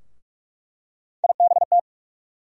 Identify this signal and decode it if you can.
Morse code, heard as a keyed tone, about 44 wpm, 700 Hz. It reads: IBT
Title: USA Call Sign Postfixes - 3 Letters 45wpm